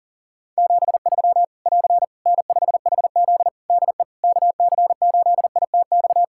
Morse code, heard as a keyed tone, about 42 wpm, 700 Hz. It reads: 73 <AR> N5HZ DE KC8ITX
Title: Practice QSO - Exchange 4 - Ending - 40wpm